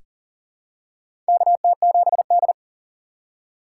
Morse code, heard as a keyed tone, about 37 wpm, 700 Hz. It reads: KT7D
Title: USA Call Signs - 4 Characters - 2x Character Spacing - 40wpm